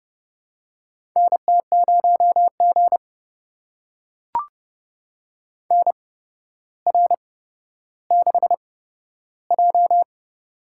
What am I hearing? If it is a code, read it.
Morse code, heard as a keyed tone, about 32 wpm, spaced as for 13 wpm, 700 Hz. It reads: NT0G NR6J
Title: USA Call Signs - 4 Characters - 8x Character Spacing - 30wpm